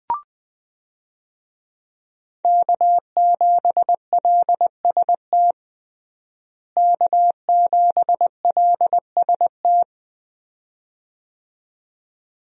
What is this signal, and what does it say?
Morse code, heard as a keyed tone, about 20 wpm, 700 Hz. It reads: K7LST K7LST
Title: USA Call Signs - 5 Characters - Repeated 2x - 20wpm